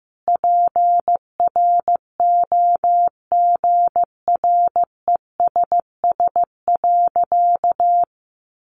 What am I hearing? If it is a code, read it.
Morse code, heard as a keyed tone, about 15 wpm, 700 Hz. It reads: PROGRESS.
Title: Sentences from Top 1400 Words 15wpm